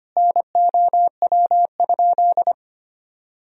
Morse code, heard as a keyed tone, about 25 wpm, 700 Hz. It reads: NOW?
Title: Sentences from Top 200 Words and Abbreviations 25wpm